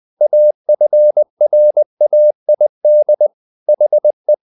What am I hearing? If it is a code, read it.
Morse code, heard as a keyed tone, about 20 wpm, 600 Hz. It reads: AFRAID HE
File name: the old man and the sea / the old man and the sea part 1 20wpm